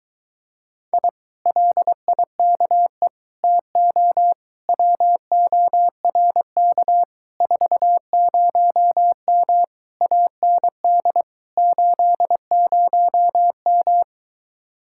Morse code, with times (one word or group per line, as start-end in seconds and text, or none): 0.9-1.1 I
1.5-3.1 LIKE
3.4-4.3 TO
4.7-7.0 WORK
7.4-9.6 40M
10.0-11.2 AND
11.6-14.0 80M